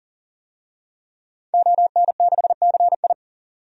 Morse code, heard as a keyed tone, about 40 wpm, 700 Hz. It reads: ON6CI
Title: DXCC-100 Call Signs - 5 Characters - 40wpm